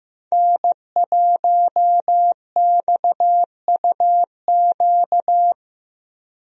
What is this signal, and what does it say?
Morse code, heard as a keyed tone, about 15 wpm, 700 Hz. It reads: N1XUQ
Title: USA Call Signs - 5 Characters - Repeated 3x - 15wpm